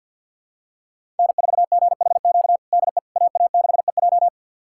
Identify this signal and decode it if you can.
Morse code, heard as a keyed tone, about 46 wpm, 700 Hz. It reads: N4GHQ DE AA6EJ
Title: USA Callsign DE Callsign - 5 Characters - 50wpm